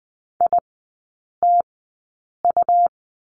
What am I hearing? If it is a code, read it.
Morse code, heard as a keyed tone, about 20 wpm, spaced as for 8 wpm, 700 Hz. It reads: ITU
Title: 178 Lesson 29e - Slash - Three Characters 20wpm